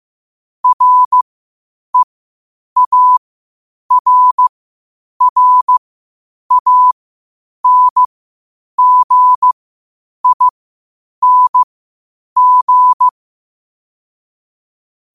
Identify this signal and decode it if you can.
Morse code, heard as a keyed tone, about 15 wpm, spaced as for 8 wpm, 1000 Hz. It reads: REARRANGING
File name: qotdfiles / 8wpm